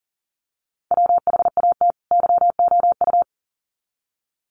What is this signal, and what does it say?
Morse code, heard as a keyed tone, about 40 wpm, 700 Hz. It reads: WHAT YOU
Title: Sets of 2 Words 40wpm